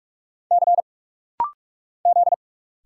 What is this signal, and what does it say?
Morse code, heard as a keyed tone, about 46 wpm, 700 Hz. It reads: C Z
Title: Single Letter-Number Alt - Rapid-Fire 45wpm